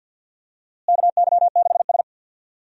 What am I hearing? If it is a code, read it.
Morse code, heard as a keyed tone, about 46 wpm, 700 Hz. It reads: KY6S
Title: USA Call Signs - 4 Characters - Repeated 2x - 50wpm